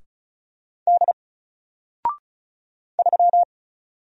Morse code, heard as a keyed tone, about 34 wpm, 700 Hz. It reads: D 3
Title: Single Letter-Number 35wpm